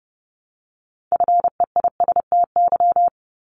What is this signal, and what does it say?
Morse code, heard as a keyed tone, about 30 wpm, 700 Hz. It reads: FEISTY
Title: Multi-letter Phonemes Repeated 3x 30wpm